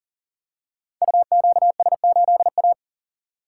Morse code, heard as a keyed tone, about 40 wpm, 700 Hz. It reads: UQS8A
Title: Code Groups - 5 Letter:Numbers - 40wpm